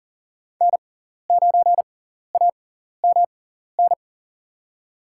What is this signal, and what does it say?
Morse code, heard as a keyed tone, about 40 wpm, spaced as for 14 wpm, 690 Hz. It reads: N9AMN
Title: USA Call Signs - 6x Character Spacing - 40wpm